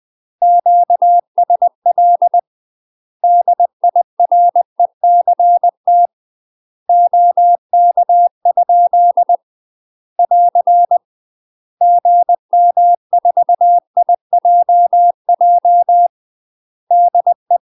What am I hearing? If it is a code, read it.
Morse code, heard as a keyed tone, about 20 wpm, 700 Hz. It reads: QSL DIRECT OK? <AR> GM4IJJ DE
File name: UK MASTER QSOs TRAINING / 20WPM / UK MASTER QSOs 0060